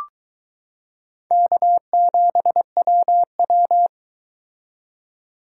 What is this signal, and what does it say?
Morse code, heard as a keyed tone, about 22 wpm, 700 Hz. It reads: K7WW